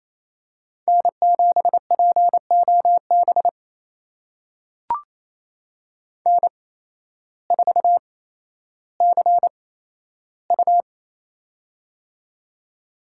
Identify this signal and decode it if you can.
Morse code, heard as a keyed tone, about 28 wpm, spaced as for 13 wpm, 700 Hz. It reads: N7POB N4CU
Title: USA Call Signs - 8x Character Spacing - 28wpm